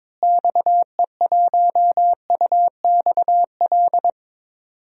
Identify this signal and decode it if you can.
Morse code, heard as a keyed tone, about 21 wpm, 700 Hz. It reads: XE1UXL
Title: DXCC-100 Call Signs - 6 Characters - Repeated 2x - 22wpm